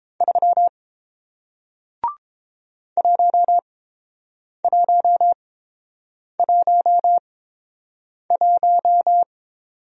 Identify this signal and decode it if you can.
Morse code, heard as a keyed tone, about 30 wpm, spaced as for 7 wpm, 700 Hz. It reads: 3 1111